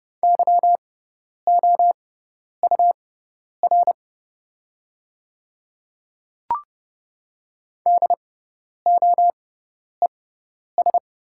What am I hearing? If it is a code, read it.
Morse code, heard as a keyed tone, about 30 wpm, spaced as for 10 wpm, 700 Hz. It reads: YOUR DOES
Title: 6x Character Spacing - Top 200 Words - 30wpm